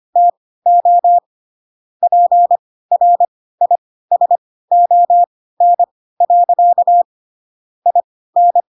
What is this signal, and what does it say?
Morse code, heard as a keyed tone, about 25 wpm, spaced as for 16 wpm, 700 Hz. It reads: TO PRISON. IN
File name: Pinocchio / Pinocchio ch4 10 25 ik1hge